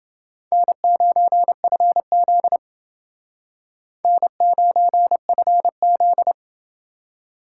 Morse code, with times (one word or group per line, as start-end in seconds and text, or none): 0.5-2.6 N9FZ
4.0-6.3 N9FZ